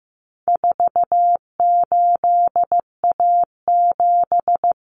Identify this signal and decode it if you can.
Morse code, heard as a keyed tone, about 15 wpm, 700 Hz. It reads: 48A7
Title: Common Cut Numbers - 4-Digit 15wpm